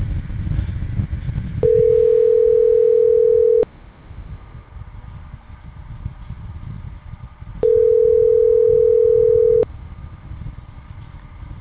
Ringback tone (North America), 440 + 480 Hz.